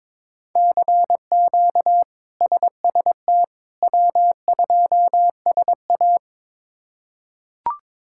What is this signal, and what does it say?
Morse code, heard as a keyed tone, about 22 wpm, 700 Hz. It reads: CQ SST W2SA
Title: K1USN SST QSO Practice - 22wpm